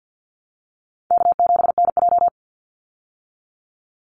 Morse code, heard as a keyed tone, about 48 wpm, 700 Hz. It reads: K7NJ
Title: Easy Call Signs Repeated 3x - 50wpm